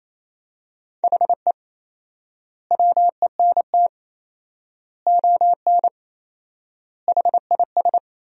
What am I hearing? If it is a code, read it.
Morse code, heard as a keyed tone, about 28 wpm, 700 Hz. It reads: HE WENT ON HIS
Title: 4x Word-Spacing - Sentences from Top 200 Words - 28wpm